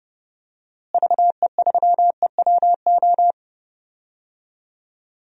Morse code, heard as a keyed tone, about 32 wpm, 700 Hz. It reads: VE3EWO
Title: DXCC-100 Call Signs - 6 Characters - Repeated 3x - 30wpm